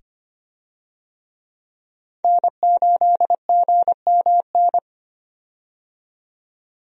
Morse code, heard as a keyed tone, about 25 wpm, 700 Hz. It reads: N8GMN
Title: USA Call Signs - 5 Characters - 25wpm